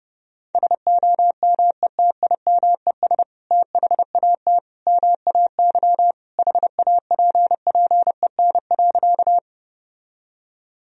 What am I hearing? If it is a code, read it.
Morse code, heard as a keyed tone, about 30 wpm, 700 Hz. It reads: SOMETIMES THAT MAY HAPPEN.
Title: Sentences from Top 1000 Words - Review Part 3 - 30wpm